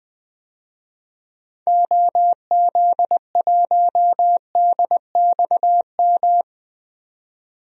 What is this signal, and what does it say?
Morse code, heard as a keyed tone, about 20 wpm, 700 Hz. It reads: OZ1DXM
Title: DXCC-100 Call Signs - 6 Characters - 20wpm